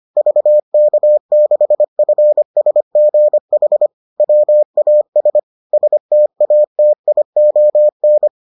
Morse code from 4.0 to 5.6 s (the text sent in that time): WAS